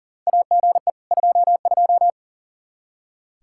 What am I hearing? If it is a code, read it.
Morse code, heard as a keyed tone, about 40 wpm, 700 Hz. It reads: AGE 22
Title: Practice QSO - Exchange 3 - More Info - 40wpm